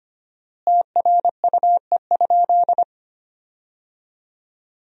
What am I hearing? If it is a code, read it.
Morse code, heard as a keyed tone, about 25 wpm, 700 Hz. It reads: TRUE?
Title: 4x Word-Spacing - Sentences from Top 400 Words - 25wpm